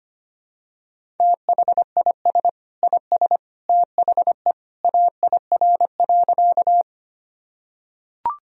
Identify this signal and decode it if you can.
Morse code, heard as a keyed tone, about 25 wpm, 700 Hz. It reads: THIS IS THE AIR.
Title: Sentences from Top 1000 Words - Review Part 1 - 25wpm